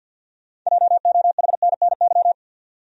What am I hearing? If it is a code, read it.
Morse code, heard as a keyed tone, about 46 wpm, 700 Hz. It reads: JOHNNY
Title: USA Common Names - 50wpm